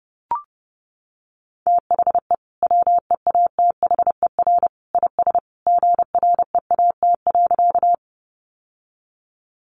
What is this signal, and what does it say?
Morse code, heard as a keyed tone, about 30 wpm, 700 Hz. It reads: THE WEATHER IS GREAT.
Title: Sentences from Top 800 Words 30wpm